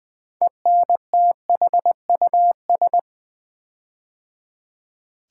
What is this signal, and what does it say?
Morse code, heard as a keyed tone, about 20 wpm, 700 Hz. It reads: ENTHUS
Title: Build A Word - 6 Letters At A Time - 12-Letter Words - 20wpm